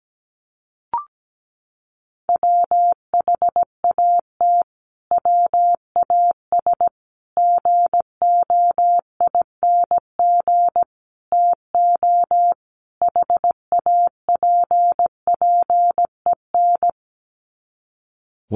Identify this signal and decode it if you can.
Morse code, heard as a keyed tone, about 17 wpm, 700 Hz. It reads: WHAT WAS GOING TO HAPPEN
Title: Sets of 5 Words Encore 17wpm